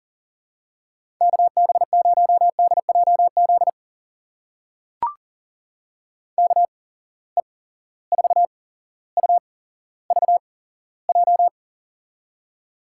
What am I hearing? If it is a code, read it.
Morse code, heard as a keyed tone, about 39 wpm, spaced as for 17 wpm, 700 Hz. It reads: KB0DJZ KE4UVJ
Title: USA Call Signs - 6 Characters - 8x Character Spacing - 40wpm